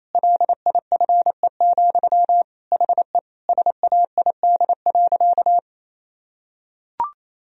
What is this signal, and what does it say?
Morse code, heard as a keyed tone, about 28 wpm, 700 Hz. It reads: LIFE, HE SAID.
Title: Sentences from Top 1000 Words - Review Part 1 - 28wpm